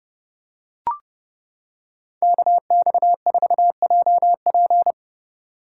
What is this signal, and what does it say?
Morse code, heard as a keyed tone, about 28 wpm, 700 Hz. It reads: KX4JP